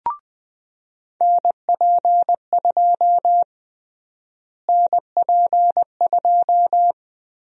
Morse code, heard as a keyed tone, about 20 wpm, 700 Hz. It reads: NP2 NP2
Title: USA Call Sign Prefixes - Repeated 2x - 20wpm